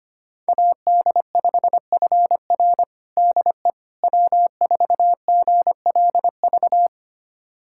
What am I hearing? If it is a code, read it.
Morse code, heard as a keyed tone, about 25 wpm, 700 Hz. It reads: AD5FR DE W4GLV
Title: USA Callsign DE Callsign - 5 Characters - 25wpm